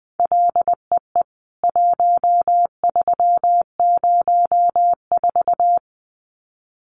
Morse code, heard as a keyed tone, about 20 wpm, 700 Hz. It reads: LEE 1304
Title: CWT Exchange 20wpm